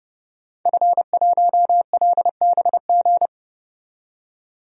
Morse code, heard as a keyed tone, about 30 wpm, 700 Hz. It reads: F1LBG
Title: DXCC-100 Call Signs - 5 Characters - 30wpm